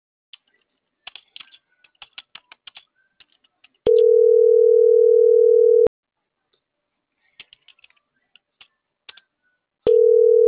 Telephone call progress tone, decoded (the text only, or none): ringback tone